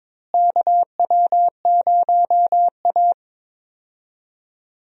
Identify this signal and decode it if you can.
Morse code, heard as a keyed tone, about 22 wpm, 700 Hz. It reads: KW0A
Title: USA Call Signs - 4 Characters - Repeated 3x - 22wpm